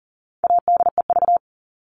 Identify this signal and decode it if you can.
Morse code, heard as a keyed tone, about 40 wpm, 700 Hz. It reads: ADEV
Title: All Cut Numbers - 4-Digit 40wpm